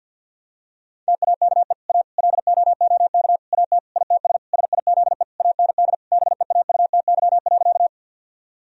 Morse code, heard as a keyed tone, about 48 wpm, 700 Hz. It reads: TAKE A LOOK AT ITS SIZE AND BEAUTY.
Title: Sentences from Top 1000 Words - Review Part 2 - 50wpm